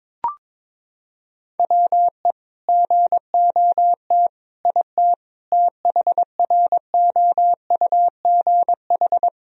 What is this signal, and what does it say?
Morse code, heard as a keyed tone, about 21 wpm, 700 Hz. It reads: WE GOT IT THROUGH
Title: Sentences from Top 1000 Words - Review Part 1 - 22wpm